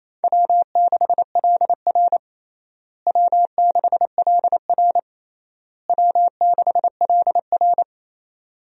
Morse code, heard as a keyed tone, about 28 wpm, 700 Hz. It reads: W6LR W6LR W6LR
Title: USA Call Signs - 4 Characters - Repeated 3x - 28wpm